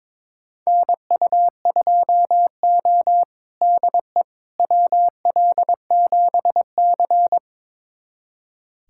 Morse code, heard as a keyed tone, about 22 wpm, 700 Hz. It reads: NU2O DE WL7C
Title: USA Callsign DE Callsign - 4 Characters - 22wpm